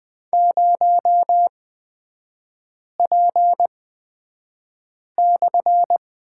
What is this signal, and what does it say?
Morse code, heard as a keyed tone, about 20 wpm, spaced as for 5 wpm, 700 Hz. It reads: 0P/
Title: 250 Lesson 40e - BK - Three Characters 20wpm